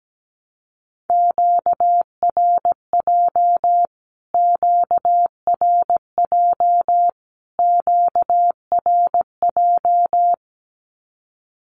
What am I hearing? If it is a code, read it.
Morse code, heard as a keyed tone, about 17 wpm, 700 Hz. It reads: QRJ QRJ QRJ
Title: Q-Codes Repeated 3x 17wpm